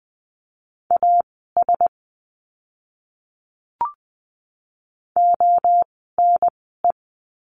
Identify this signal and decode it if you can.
Morse code, heard as a keyed tone, about 20 wpm, spaced as for 14 wpm, 700 Hz. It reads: AS ONE
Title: 185 Lesson 30f - Letter b - Words 20wpm